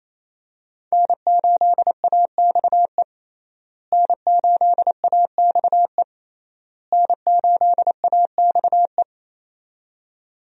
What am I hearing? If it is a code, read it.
Morse code, heard as a keyed tone, about 28 wpm, 700 Hz. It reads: N8AXE N8AXE N8AXE
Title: USA Call Signs - 5 Characters - Repeated 3x - 28wpm